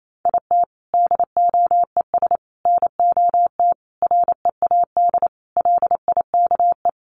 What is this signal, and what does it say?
Morse code, heard as a keyed tone, about 28 wpm, 700 Hz. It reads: IT DOES NOT READ LIKE
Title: Sentences from Top 800 Words 28wpm